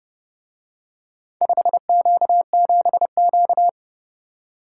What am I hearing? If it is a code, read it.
Morse code, heard as a keyed tone, about 28 wpm, 700 Hz. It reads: 5Q7Q